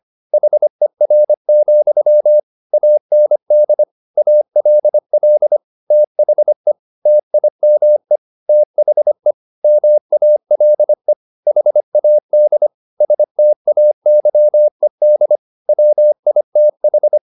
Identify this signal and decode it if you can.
Morse code, heard as a keyed tone, about 25 wpm, 600 Hz. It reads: HER, AND ALL THE TIME THE MALE HAD STAYED WITH
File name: the old man and the sea part 1 25wpm